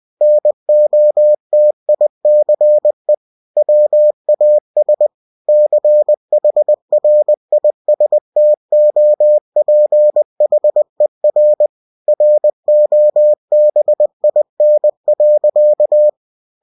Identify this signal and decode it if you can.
Morse code, heard as a keyed tone, about 20 wpm, 600 Hz. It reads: NOTICE WAS CHRISTOPHER ROBIN.
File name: winnie-20wpm